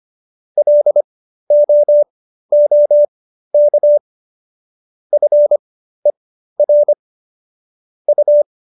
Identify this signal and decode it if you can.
Morse code, heard as a keyed tone, about 25 wpm, spaced as for 13 wpm, 590 Hz. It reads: LOOK FER U